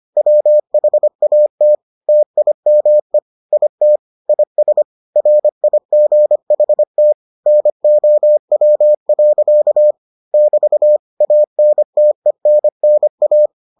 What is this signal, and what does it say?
Morse code, heard as a keyed tone, about 25 wpm, 600 Hz. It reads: WHAT TIME IT IS RIGHT NOW. <BT> ANTENNA